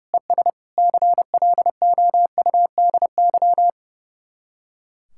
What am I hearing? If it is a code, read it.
Morse code, heard as a keyed tone, about 37 wpm, 700 Hz. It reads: ES CLOUDY
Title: Speed-Racing Common QSO Phrases - 30wpm